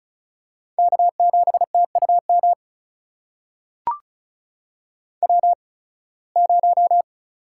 Morse code, heard as a keyed tone, about 36 wpm, spaced as for 23 wpm, 700 Hz. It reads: K7TUM W0
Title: USA Call Signs - 5 Characters - 8x Character Spacing - 35wpm